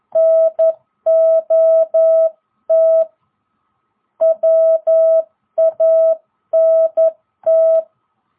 Morse code, heard as a keyed tone, about 11 wpm, 660 Hz. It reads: NOT WANT